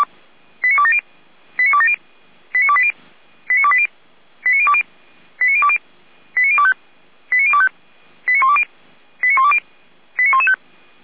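Five-tone selective call (ZVEI), heard as quick runs of five tones, tones of about 70 ms. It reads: 89270 89270 89280 89280 89020 89020 89025 89025 89120 89120 89115
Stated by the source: ZVEI five-tone selective call received on SDR.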